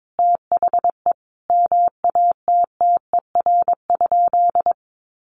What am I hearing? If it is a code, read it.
Morse code, heard as a keyed tone, about 22 wpm, 700 Hz. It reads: THE MATTER?
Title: Sentences from Top 1300 Words 22wpm